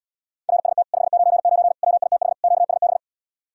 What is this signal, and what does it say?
Morse code, heard as a keyed tone, about 80 wpm, 700 Hz. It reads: RST 599 WITH QSB